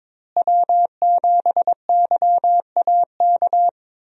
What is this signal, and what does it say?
Morse code, heard as a keyed tone, about 22 wpm, 700 Hz. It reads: W7YAK
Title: USA Call Signs - 5 Characters - Repeated 3x - 22wpm